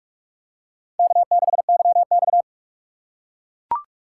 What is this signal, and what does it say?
Morse code, heard as a keyed tone, about 41 wpm, 700 Hz. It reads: K6YX